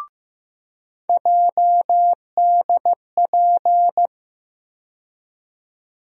Morse code, heard as a keyed tone, about 16 wpm, 700 Hz. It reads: JDP
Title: USA Call Sign Postfixes - 3 Letters 15wpm